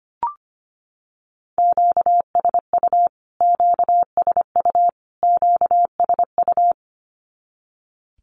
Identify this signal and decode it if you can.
Morse code, heard as a keyed tone, about 25 wpm, 700 Hz. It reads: QSU QSU QSU
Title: Q-Codes Repeated 3x 25wpm